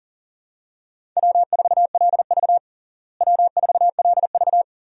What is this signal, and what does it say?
Morse code, heard as a keyed tone, about 40 wpm, 700 Hz. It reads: W4LV W4LV
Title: USA Call Signs - 4 Characters - Repeated 2x - 40wpm